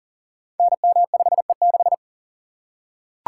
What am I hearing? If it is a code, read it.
Morse code, heard as a keyed tone, about 41 wpm, 700 Hz. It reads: NM5E6
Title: Code Groups - 5 Letter:Numbers - 40wpm